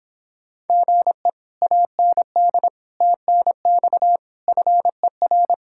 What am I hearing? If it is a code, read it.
Morse code, heard as a keyed tone, about 26 wpm, 700 Hz. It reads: GE AND TNX FER